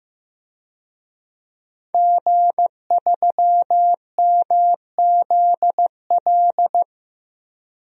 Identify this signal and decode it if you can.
Morse code, heard as a keyed tone, about 15 wpm, 700 Hz. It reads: G3MZL
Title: DXCC-100 Call Signs - 5 Characters - Repeated 3x - 15wpm